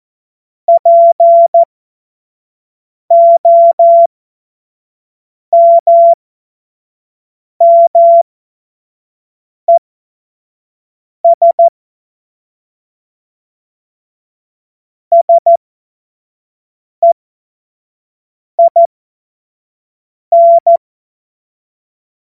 Morse code, heard as a keyed tone, about 14 wpm, spaced as for 5 wpm, 680 Hz. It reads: POMMES SEIN
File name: Morsekurs / Witze / Witz 11. Mai 2021, 25 F70